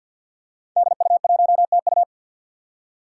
Morse code, heard as a keyed tone, about 51 wpm, 700 Hz. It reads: DU1TU
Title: Five Challenging Letter-Number Combinations 50wpm